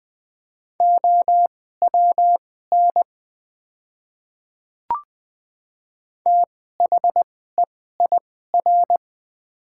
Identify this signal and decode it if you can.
Morse code, heard as a keyed tone, about 20 wpm, spaced as for 14 wpm, 690 Hz. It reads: OWN THEIR
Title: 2x Character Spacing - Top 500 Words - 20wpm